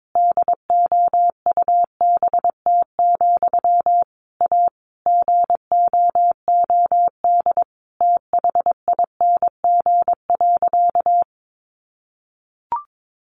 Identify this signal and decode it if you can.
Morse code, heard as a keyed tone, about 22 wpm, 700 Hz. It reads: DOUBT, A GOOD THING.
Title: Sentences from Top 1500 Words 22wpm